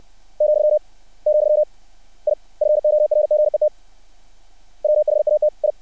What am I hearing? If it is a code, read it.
Morse code, heard as a keyed tone, about 60 wpm, 600 Hz. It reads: , , I FORGET WHAT I
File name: archived / CRSnet